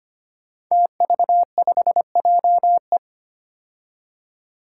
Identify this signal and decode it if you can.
Morse code, heard as a keyed tone, about 25 wpm, 700 Hz. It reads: TV5JE
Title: Five Challenging Letter-Number Combinations 25wpm